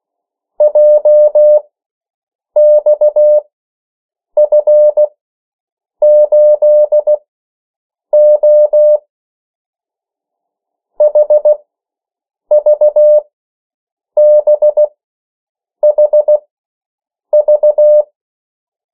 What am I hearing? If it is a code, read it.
Morse code, heard as a keyed tone, about 16 wpm, spaced as for 7 wpm, 600 Hz. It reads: JXF8O HVBHV
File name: morse mp3 / 14 urok 13